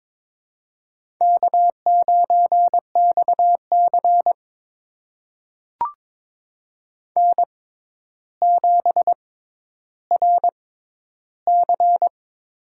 Morse code, heard as a keyed tone, about 22 wpm, spaced as for 11 wpm, 700 Hz. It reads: K9XC N7RC
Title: USA Call Signs - 4 Characters - 6x Character Spacing - 22wpm